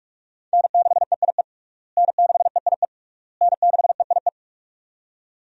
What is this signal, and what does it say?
Morse code, heard as a keyed tone, about 46 wpm, 700 Hz. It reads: N6EIE N6EIE N6EIE
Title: USA Call Signs - 5 Characters - Repeated 3x - 45wpm